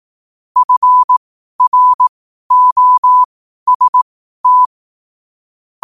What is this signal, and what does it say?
Morse code, heard as a keyed tone, about 18 wpm, spaced as for 12 wpm, 1000 Hz. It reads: FROST